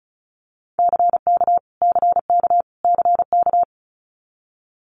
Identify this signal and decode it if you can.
Morse code, heard as a keyed tone, about 34 wpm, 700 Hz. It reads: CK CK CK
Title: Multi-letter Phonemes Repeated 3x 35wpm